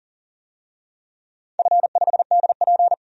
Morse code, heard as a keyed tone, about 40 wpm, 700 Hz. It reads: F5DP
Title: DXCC-100 Call Signs - 4 Characters - 40wpm